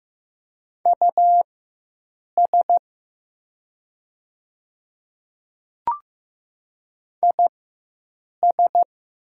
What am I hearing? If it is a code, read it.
Morse code, heard as a keyed tone, about 15 wpm, spaced as for 7 wpm, 700 Hz. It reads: US IS
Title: 4x Character Spacing - Top 200 Words - 15wpm